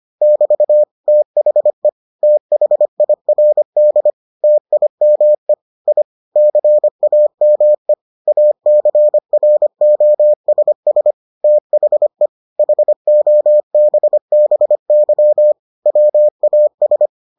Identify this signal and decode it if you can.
Morse code, heard as a keyed tone, about 25 wpm, 600 Hz. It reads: <BT> THE THIRD TIME I CAME ACROSS THE HOBBY WAS